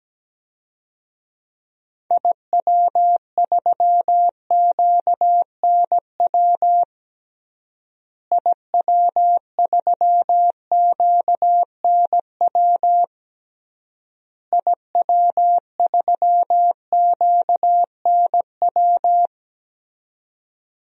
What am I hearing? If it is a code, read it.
Morse code, heard as a keyed tone, about 17 wpm, 700 Hz. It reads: IW3QNW IW3QNW IW3QNW
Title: DXCC-100 Call Signs - 6 Characters - Repeated 3x - 17wpm